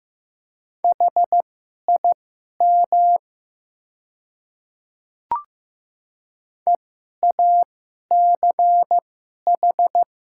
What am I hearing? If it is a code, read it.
Morse code, heard as a keyed tone, about 15 wpm, spaced as for 11 wpm, 690 Hz. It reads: HIM EACH
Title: 2x Character Spacing - Top 100 Words - 15wpm